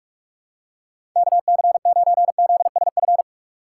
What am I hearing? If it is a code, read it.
Morse code, heard as a keyed tone, about 46 wpm, 700 Hz. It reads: KC9ZSF
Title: USA Call Signs - 6 Characters - 2x Character Spacing - 45wpm